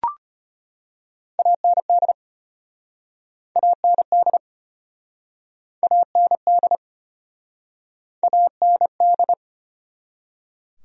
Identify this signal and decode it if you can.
Morse code, heard as a keyed tone, about 37 wpm, 700 Hz. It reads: AND AND AND AND